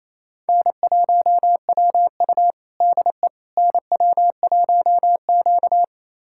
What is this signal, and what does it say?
Morse code, heard as a keyed tone, about 28 wpm, 700 Hz. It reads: N1WU DE NW1Q
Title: USA Callsign DE Callsign - 4 Characters - 28wpm